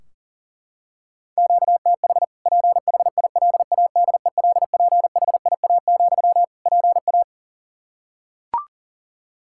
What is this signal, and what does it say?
Morse code, heard as a keyed tone, about 40 wpm, 700 Hz. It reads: QTH PHILADELPHIA, PA
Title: Practice QSO - Exchange 1 - Beginning - 40wpm